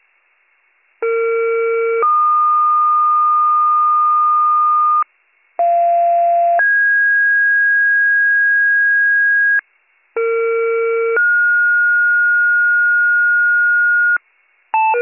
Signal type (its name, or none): two-tone page